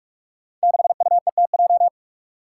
Morse code, heard as a keyed tone, about 41 wpm, 700 Hz. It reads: 6UETJ
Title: Five Challenging Letter-Number Combinations 45wpm